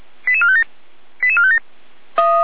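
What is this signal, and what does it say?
Five-tone selective call (ZVEI), heard as quick runs of five tones, tones of about 70 ms. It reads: 88457 88457
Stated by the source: ZVEI five-tone call 88305 received on an SDR.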